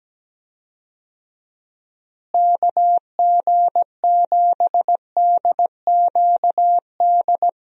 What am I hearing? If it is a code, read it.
Morse code, heard as a keyed tone, about 17 wpm, 700 Hz. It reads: KG7DQD
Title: USA Call Signs - 6 Characters - Repeated 2x - 17wpm